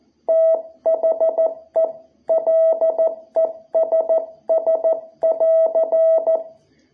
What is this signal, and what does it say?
Morse code, heard as a keyed tone, about 14 wpm, 660 Hz. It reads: THE LESS <AR>
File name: jh8cbh / 2021cw